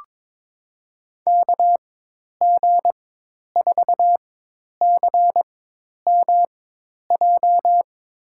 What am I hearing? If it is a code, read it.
Morse code, heard as a keyed tone, about 21 wpm, spaced as for 10 wpm, 700 Hz. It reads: KG4CMJ
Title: USA Call Signs - 6 Characters - 4x Character Spacing - 22wpm